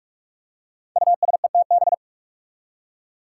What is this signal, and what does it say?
Morse code, heard as a keyed tone, about 44 wpm, 700 Hz. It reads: USETB
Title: Five Challenging Letter-Number Combinations 45wpm